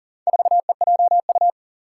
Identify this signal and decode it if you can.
Morse code, heard as a keyed tone, about 40 wpm, 700 Hz. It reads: 4EJU
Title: Four Challenging Letter-Number Combinations 40wpm